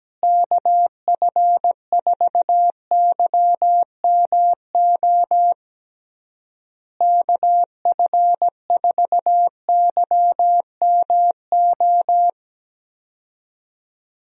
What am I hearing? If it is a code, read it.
Morse code, heard as a keyed tone, about 17 wpm, 700 Hz. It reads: KF4YMO KF4YMO
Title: USA Call Signs - 6 Characters - Repeated 3x - 17wpm